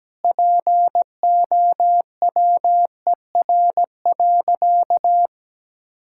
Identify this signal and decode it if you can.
Morse code, heard as a keyed tone, about 17 wpm, 700 Hz. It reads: POWER.
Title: Sentences from Top 1600 Words 17wpm